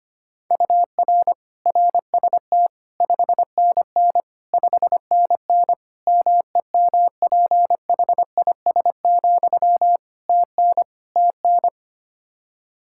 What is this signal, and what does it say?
Morse code, heard as a keyed tone, about 25 wpm, 700 Hz. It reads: UR RST 5NN 5NN MEMPHIS, TN TN
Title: Practice QSO - Exchange 1 - Beginning - 25wpm